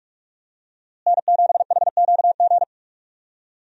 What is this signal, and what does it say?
Morse code, heard as a keyed tone, about 46 wpm, 700 Hz. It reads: N7HQG
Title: USA Call Signs - 5 Characters - Repeated 3x - 45wpm